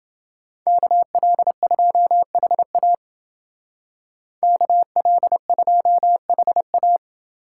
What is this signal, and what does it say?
Morse code, heard as a keyed tone, about 28 wpm, 700 Hz. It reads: KL2HA KL2HA